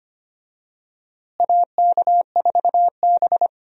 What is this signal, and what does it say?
Morse code, heard as a keyed tone, about 25 wpm, 700 Hz. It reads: AK4B
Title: USA Call Signs - 4 Characters - Repeated 3x - 25wpm